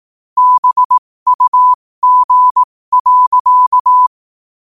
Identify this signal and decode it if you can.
Morse code, heard as a keyed tone, about 18 wpm, spaced as for 15 wpm, 1000 Hz. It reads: BUG.
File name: qotdfiles / 15wpm